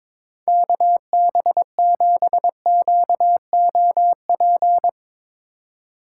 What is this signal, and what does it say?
Morse code, heard as a keyed tone, about 22 wpm, 700 Hz. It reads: KB7QOP
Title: K1USN SST QSO Practice - 22wpm